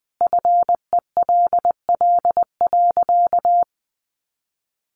Morse code, heard as a keyed tone, about 20 wpm, 700 Hz. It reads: FELL.
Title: A Sound Of Thunder 20wpm